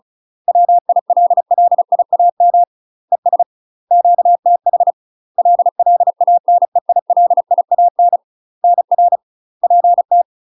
Morse code, heard as a keyed tone, about 36 wpm, 700 Hz. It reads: WILLIAM ES QTH LLANEILIAN NR PT
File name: UK MASTER QSOs TRAINING / 35WPM / UK MASTER QSOs 0016